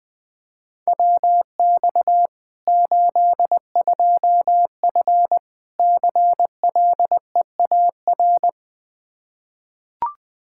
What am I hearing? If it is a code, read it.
Morse code, heard as a keyed tone, about 20 wpm, 700 Hz. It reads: WX 82F CLEAR
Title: Practice QSO - Exchange 2 - Info - 20wpm